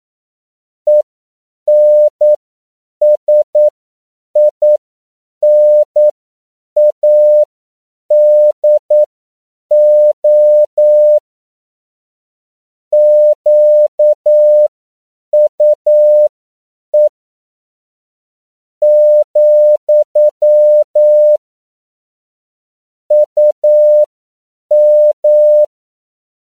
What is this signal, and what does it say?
Morse code, heard as a keyed tone, about 9 wpm, spaced as for 7 wpm, 600 Hz. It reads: ENSINADO QUE, UM